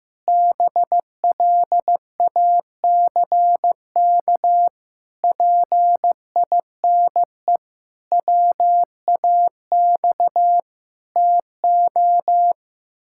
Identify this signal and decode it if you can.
Morse code, heard as a keyed tone, about 15 wpm, 700 Hz. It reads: BLACK PINE WAX TO
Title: Pangram Challenge v2 - 15wpm